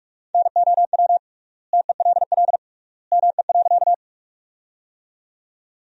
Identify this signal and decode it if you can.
Morse code, heard as a keyed tone, about 41 wpm, 700 Hz. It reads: NOW TELL ME.
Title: 3x Word-Spacing - Sentences from Top 200 Words - 45wpm